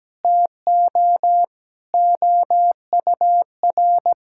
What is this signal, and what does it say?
Morse code, heard as a keyed tone, about 17 wpm, 700 Hz. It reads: TO OUR
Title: Sentences from Top 1000 Words - Review Part 6 - 17wpm